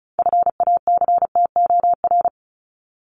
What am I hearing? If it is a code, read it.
Morse code, heard as a keyed tone, about 34 wpm, 700 Hz. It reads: FACTOR
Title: ES:S Suffix - Top 100 Words with Base Words 35wpm